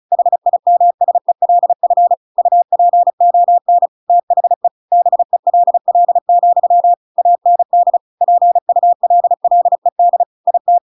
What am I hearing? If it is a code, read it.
Morse code, heard as a keyed tone, about 34 wpm, 700 Hz. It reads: HIMSELF UPON THE BELL, AND PULLED IT